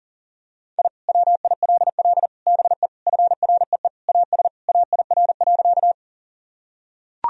Morse code, heard as a keyed tone, about 40 wpm, 700 Hz. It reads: I WILL BE FREE AS AIR.
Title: Sentences from Top 1000 Words - Review Part 1 - 40wpm